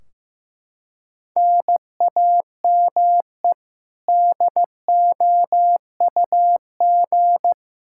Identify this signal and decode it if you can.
Morse code, heard as a keyed tone, about 15 wpm, 700 Hz. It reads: NAME DOUG
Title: Practice QSO - Exchange 1 - Beginning - 15wpm